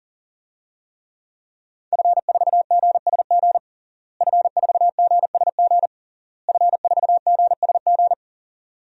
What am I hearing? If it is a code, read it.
Morse code, heard as a keyed tone, about 40 wpm, 700 Hz. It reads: F4GSG F4GSG F4GSG
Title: DXCC-100 Call Signs - 5 Characters - Repeated 3x - 40wpm